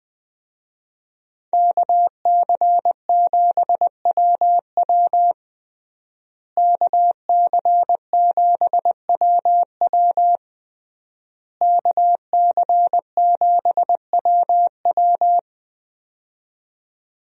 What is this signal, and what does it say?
Morse code, heard as a keyed tone, about 20 wpm, 700 Hz. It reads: KC7WW KC7WW KC7WW
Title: USA Call Signs - 5 Characters - Repeated 3x - 20wpm